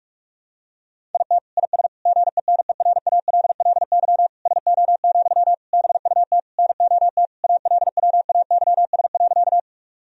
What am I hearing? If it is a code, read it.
Morse code, heard as a keyed tone, about 46 wpm, 700 Hz. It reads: IT IS GENERALLY SO, BUT NOT ALWAYS.
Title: Sentences from Top 1000 Words - Review Part 4 - 45wpm